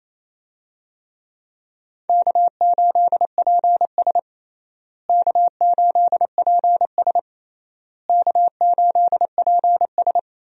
Morse code, heard as a keyed tone, about 28 wpm, 700 Hz. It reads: K8PS K8PS K8PS
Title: USA Call Signs - 4 Characters - Repeated 3x - 28wpm